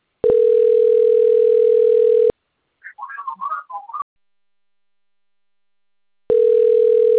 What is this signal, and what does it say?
Ringback tone (North America), 440 + 480 Hz.